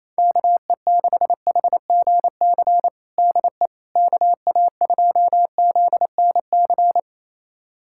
Morse code, heard as a keyed tone, about 28 wpm, 700 Hz. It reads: KE6HGC DE KA2ZNC
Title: USA Callsign DE Callsign - 6 Characters - 28wpm